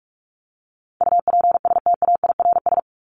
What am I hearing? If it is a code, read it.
Morse code, heard as a keyed tone, about 44 wpm, 700 Hz. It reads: UPSTAIRS
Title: Top 1751-2000 Words 45wpm